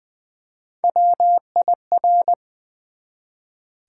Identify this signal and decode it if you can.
Morse code, heard as a keyed tone, about 20 wpm, 700 Hz. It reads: WIR
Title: Build A Word - 1 Letter At A Time - 4-Letter Words - 20wpm